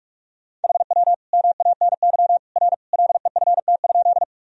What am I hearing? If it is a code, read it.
Morse code, heard as a keyed tone, about 41 wpm, 700 Hz. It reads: HW MANY R LEFT?
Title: Sentences from Top 300 Words and Abbreviations 45wpm